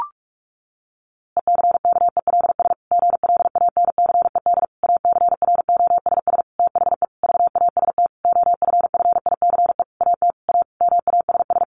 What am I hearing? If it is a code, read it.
Morse code, heard as a keyed tone, about 44 wpm, 700 Hz. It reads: ECKELS GLANCED ACROSS THE VAST OFFICE AT A MASS
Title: A Sound Of Thunder 45wpm